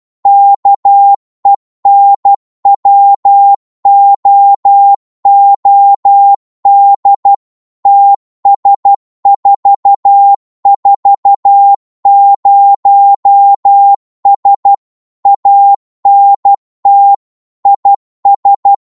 Morse code, heard as a keyed tone, about 12 wpm, 800 Hz. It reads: KENWOOD TS440S ANT IS
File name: Morse / qso-12wpm